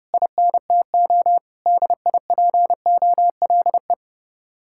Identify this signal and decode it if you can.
Morse code, heard as a keyed tone, about 31 wpm, 700 Hz. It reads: INTO DIPOLE